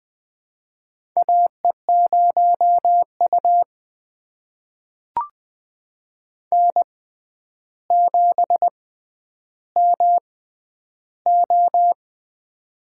Morse code, heard as a keyed tone, about 20 wpm, spaced as for 10 wpm, 700 Hz. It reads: AE0U N7MO
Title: USA Call Signs - 4 Characters - 6x Character Spacing - 20wpm